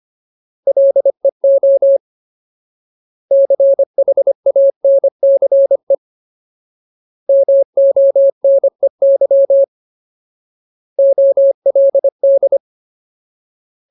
Morse code, heard as a keyed tone, about 25 wpm, 550 Hz. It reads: LEO CHANCE MONEY OLD
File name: Ultimate Word List / 25WPM Ultimate